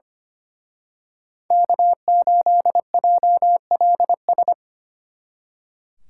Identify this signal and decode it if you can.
Morse code, heard as a keyed tone, about 31 wpm, 700 Hz. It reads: K8JLS
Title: Speed Racing - USA Call Signs 25wpm